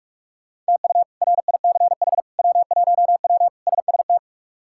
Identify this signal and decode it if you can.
Morse code, heard as a keyed tone, about 42 wpm, 700 Hz. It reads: TU RICH W1W SST
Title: K1USN SST QSO Practice - 45wpm